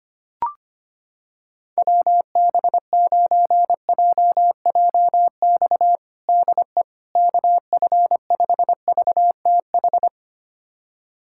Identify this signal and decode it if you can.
Morse code, heard as a keyed tone, about 25 wpm, 700 Hz. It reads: WB9JJX DE KF5VTH
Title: USA Callsign DE Callsign - 6 Characters - 25wpm